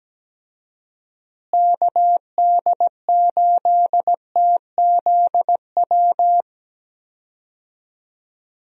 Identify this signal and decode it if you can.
Morse code, heard as a keyed tone, about 17 wpm, 700 Hz. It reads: KD8TZW
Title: USA Call Signs - 6 Characters - Repeated 3x - 17wpm